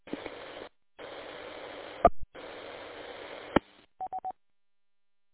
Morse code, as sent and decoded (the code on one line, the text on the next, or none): none